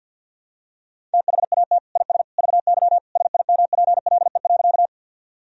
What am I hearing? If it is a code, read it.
Morse code, heard as a keyed tone, about 52 wpm, 700 Hz. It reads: THAT IS VY SIMPLE.
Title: Sentences from Top 400 Words and Abbreviations 50wpm